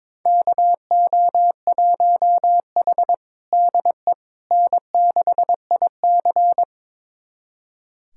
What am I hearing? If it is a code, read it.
Morse code, heard as a keyed tone, about 22 wpm, 700 Hz. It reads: KO1H DE N6IC
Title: USA Callsign DE Callsign - 4 Characters - 22wpm